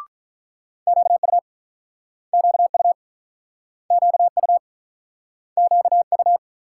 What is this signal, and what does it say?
Morse code, heard as a keyed tone, about 48 wpm, 700 Hz. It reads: QU QU QU QU